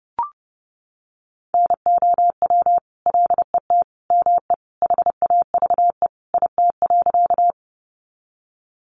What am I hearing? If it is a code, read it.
Morse code, heard as a keyed tone, about 30 wpm, 700 Hz. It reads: NOW LET ME HAVE IT.
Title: Sentences from Top 300 Words 30wpm